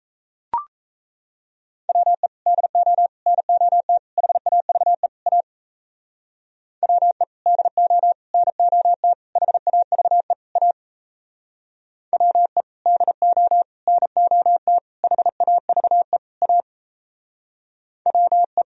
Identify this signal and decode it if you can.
Morse code, heard as a keyed tone, about 36 wpm, 700 Hz. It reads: WE DO NOT HAVE A WE DO NOT HAVE A WE DO NOT HAVE A WE